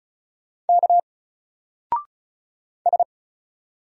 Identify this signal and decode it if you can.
Morse code, heard as a keyed tone, about 33 wpm, 700 Hz. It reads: K S
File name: Single-Letter-Number-35wpm